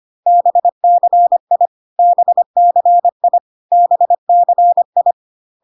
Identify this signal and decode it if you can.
Morse code, heard as a keyed tone, about 25 wpm, 700 Hz. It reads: BCI BCI BCI
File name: cw / CW common abbreviations alphabetical repeated 3 times 25 25wpm ik1hge